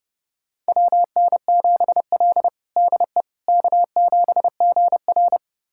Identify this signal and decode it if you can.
Morse code, heard as a keyed tone, about 32 wpm, 700 Hz. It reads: WN7L DE K7GR
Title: USA Callsign DE Callsign - 4 Characters - 30wpm